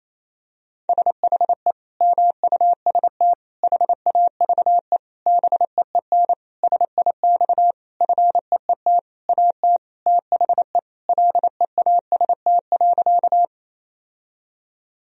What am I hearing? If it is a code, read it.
Morse code, heard as a keyed tone, about 28 wpm, 700 Hz. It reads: SHE MUST HAVE BEEN SIX FEET AT THE LEAST.
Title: Sentences from Top 1000 Words - Review Part 2 - 28wpm